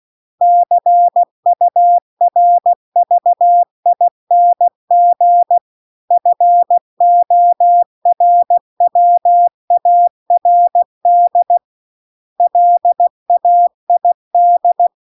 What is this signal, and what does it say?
Morse code, heard as a keyed tone, about 16 wpm, 700 Hz. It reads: CURVING FORWARD LAID